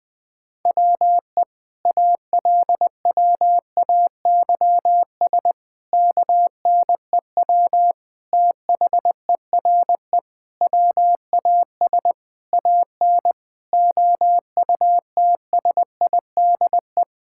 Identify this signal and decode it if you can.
Morse code, heard as a keyed tone, about 21 wpm, 700 Hz. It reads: WE ALWAYS KNEW THERE WAS AN OUTSIDE
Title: Sentences from Top 1000 Words - Review Part 6 - 20wpm